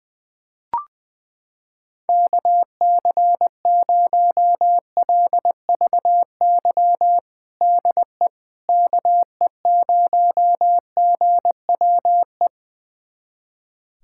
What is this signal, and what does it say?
Morse code, heard as a keyed tone, about 20 wpm, 700 Hz. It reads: KC0LVY DE KE0GWE
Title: USA Callsign DE Callsign 20wpm